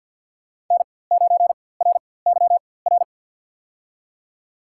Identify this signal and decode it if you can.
Morse code, heard as a keyed tone, about 48 wpm, spaced as for 23 wpm, 700 Hz. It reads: N9RYR
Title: USA Call Signs - 4x Character Spacing - 50wpm